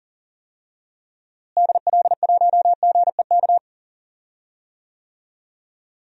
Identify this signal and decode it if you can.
Morse code, heard as a keyed tone, about 40 wpm, 700 Hz. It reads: DL1GEK
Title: DXCC-100 Call Signs - 6 Characters - Repeated 2x - 40wpm